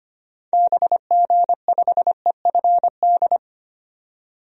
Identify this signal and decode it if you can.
Morse code, heard as a keyed tone, about 25 wpm, 700 Hz. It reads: BG5EFD
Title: DXCC-100 Call Signs - 6 Characters - 25wpm